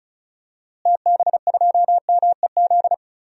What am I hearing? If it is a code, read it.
Morse code, heard as a keyed tone, about 36 wpm, 700 Hz. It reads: TB2MEZ
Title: DXCC-100 Call Signs - 6 Characters - Repeated 3x - 35wpm